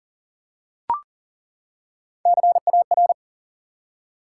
Morse code, heard as a keyed tone, about 39 wpm, 700 Hz. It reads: CAR
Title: Three Letter Words - 40wpm